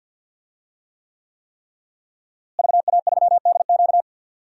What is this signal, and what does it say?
Morse code, heard as a keyed tone, about 52 wpm, 700 Hz. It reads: VA3DQ
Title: DXCC-100 Call Signs - 5 Characters - Repeated 3x - 50wpm